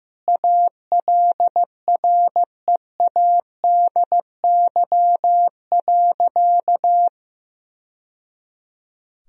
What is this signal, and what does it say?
Morse code, heard as a keyed tone, about 15 wpm, 700 Hz. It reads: ALREADY.
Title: Sentences from Top 1000 Words - Review Part 2 - 15wpm